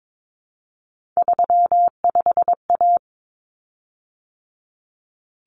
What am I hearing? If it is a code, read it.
Morse code, heard as a keyed tone, about 22 wpm, 700 Hz. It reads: 35A
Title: Common Cut Numbers - 3-Digit 22wpm